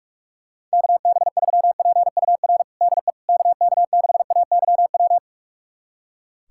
Morse code, heard as a keyed tone, about 46 wpm, 700 Hz. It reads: KB3PUR DE KK6AYW
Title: USA Callsign DE Callsign - 6 Characters - 45wpm